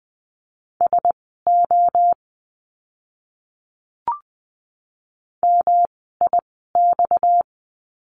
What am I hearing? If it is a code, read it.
Morse code, heard as a keyed tone, about 20 wpm, spaced as for 14 wpm, 700 Hz. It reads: SO MIX
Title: 230 Lesson 37f - Letter x - Words 20wpm